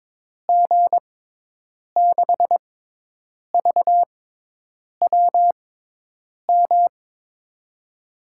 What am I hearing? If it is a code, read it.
Morse code, heard as a keyed tone, about 22 wpm, spaced as for 8 wpm, 700 Hz. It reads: G6VWM
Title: USA Call Signs - 6 Characters - 6x Character Spacing - 22wpm